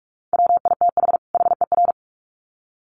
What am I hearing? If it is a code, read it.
Morse code, heard as a keyed tone, about 44 wpm, 700 Hz. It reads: WITH HER
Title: Sets of 2 Words 45wpm